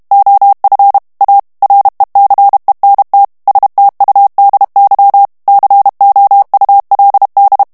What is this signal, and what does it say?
Morse code, heard as a keyed tone, about 32 wpm, 800 Hz. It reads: OF A RECENT STUDY COULD
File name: morse mp3 / ANGLAIS texte clair 1 vitesse 30 mots minute